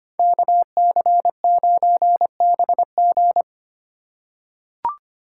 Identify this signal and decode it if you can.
Morse code, heard as a keyed tone, about 25 wpm, 700 Hz. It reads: KC9BG
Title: K1USN SST QSO Practice - 25wpm